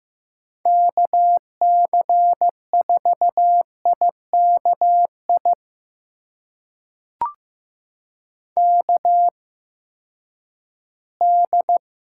Morse code, heard as a keyed tone, about 15 wpm, spaced as for 10 wpm, 700 Hz. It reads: KC4IKI KD
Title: USA Call Signs - 6 Characters - 8x Character Spacing - 15wpm